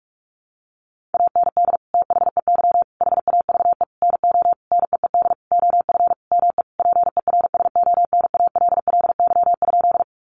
Morse code, heard as a keyed tone, about 44 wpm, 700 Hz. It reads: AND THEY HAVE NO NEED OF ME PERSONALLY?
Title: Sentences from Top 1400 Words 45wpm